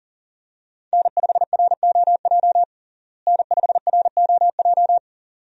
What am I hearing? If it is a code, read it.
Morse code, heard as a keyed tone, about 40 wpm, 700 Hz. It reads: N5ROJ N5ROJ
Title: USA Call Signs - 5 Characters - Repeated 2x - 40wpm